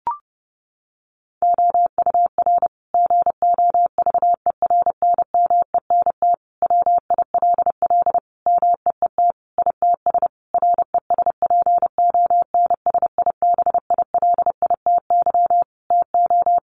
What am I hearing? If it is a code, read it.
Morse code, heard as a keyed tone, about 30 wpm, 700 Hz. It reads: OUR GOVERNMENT WILL MEET ITS RESPONSIBILITY TO
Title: Sentences from Top 1200 Words 30wpm